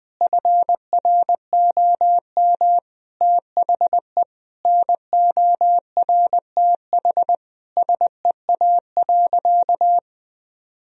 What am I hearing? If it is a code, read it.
Morse code, heard as a keyed tone, about 21 wpm, 700 Hz. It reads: FROM THE NORTH SEA.
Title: Sentences from Top 1000 Words - Review Part 1 - 20wpm